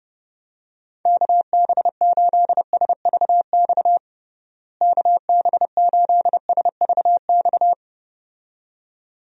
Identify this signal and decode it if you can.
Morse code, heard as a keyed tone, about 32 wpm, 700 Hz. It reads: KB8SVX KB8SVX
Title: USA Call Signs - 6 Characters - Repeated 2x - 30wpm